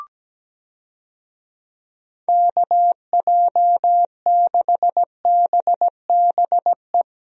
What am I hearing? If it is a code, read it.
Morse code, heard as a keyed tone, about 17 wpm, 700 Hz. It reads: KJ6BBE
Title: USA Call Signs - 6 Characters - Repeated 3x - 17wpm